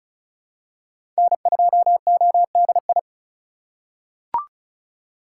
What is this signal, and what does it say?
Morse code, heard as a keyed tone, about 36 wpm, 700 Hz. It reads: N2ODI
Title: USA Call Signs - 5 Characters - 2x Character Spacing - 35wpm